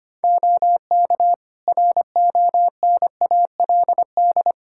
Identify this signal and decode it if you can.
Morse code, heard as a keyed tone, about 25 wpm, 700 Hz. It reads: OK RONALD
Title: Practice QSO - Exchange 2 - Info - 25wpm